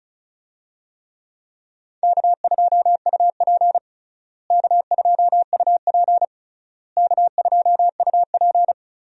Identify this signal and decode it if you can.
Morse code, heard as a keyed tone, about 36 wpm, 700 Hz. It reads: K2UP K2UP K2UP
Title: USA Call Signs - 4 Characters - Repeated 3x - 35wpm